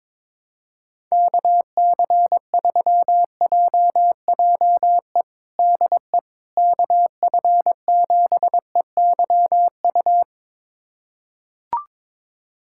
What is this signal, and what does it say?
Morse code, heard as a keyed tone, about 22 wpm, 700 Hz. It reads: KC3JJE DE KF7EYU
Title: USA Callsign DE Callsign 22wpm